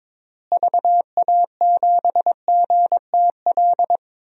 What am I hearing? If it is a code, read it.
Morse code, heard as a keyed tone, about 22 wpm, 700 Hz. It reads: VA7GTL
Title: DXCC-100 Call Signs - 6 Characters - Repeated 3x - 22wpm